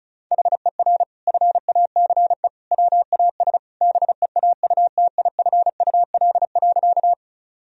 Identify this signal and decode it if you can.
Morse code, heard as a keyed tone, about 36 wpm, 700 Hz. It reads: HER FACE WAS BEAUTIFUL.
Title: Sentences from Top 1000 Words - Review Part 2 - 35wpm